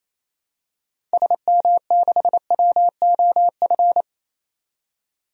Morse code, heard as a keyed tone, about 28 wpm, 700 Hz. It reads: SM6WOF
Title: DXCC-100 Call Signs - 6 Characters - 28wpm